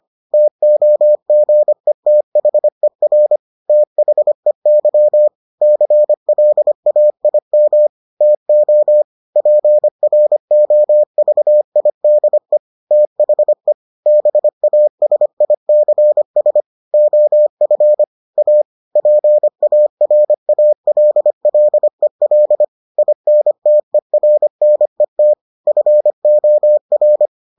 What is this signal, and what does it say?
Morse code, heard as a keyed tone, about 25 wpm, 600 Hz. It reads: TOGETHER THEY CLAIM TO PROVIDE THE BASICS OF A PARALLEL INTERNET FOR